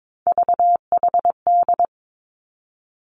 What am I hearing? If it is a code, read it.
Morse code, heard as a keyed tone, about 22 wpm, 700 Hz. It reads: VHD
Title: Three Challenging Letter-Number Combinations 22wpm